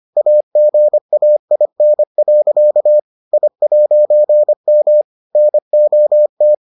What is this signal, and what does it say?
Morse code, heard as a keyed tone, about 25 wpm, 600 Hz. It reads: AGAIN. I'M NOT